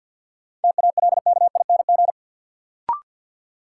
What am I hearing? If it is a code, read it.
Morse code, heard as a keyed tone, about 47 wpm, 700 Hz. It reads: TALKING
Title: Sets of 2 Words - New Word Drill - 50wpm